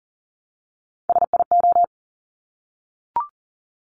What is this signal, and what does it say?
Morse code, heard as a keyed tone, about 41 wpm, 690 Hz. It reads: SIO
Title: Top Three Letter Combinations in Words 40wpm